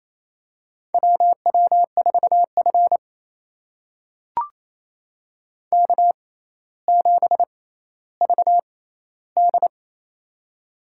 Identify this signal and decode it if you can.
Morse code, heard as a keyed tone, about 27 wpm, spaced as for 14 wpm, 700 Hz. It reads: WW4F K7VD
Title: USA Call Signs - 4 Characters - 6x Character Spacing - 28wpm